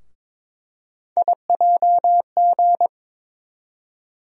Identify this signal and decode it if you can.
Morse code, heard as a keyed tone, about 21 wpm, 700 Hz. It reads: IJG
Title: USA Call Sign Postfixes - 3 Letters 22wpm